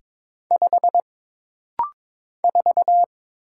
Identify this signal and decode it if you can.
Morse code, heard as a keyed tone, about 22 wpm, 700 Hz. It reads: 5 4
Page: This rapid-fire timing is geared to push your ICR (Instant Character Recognition) skills.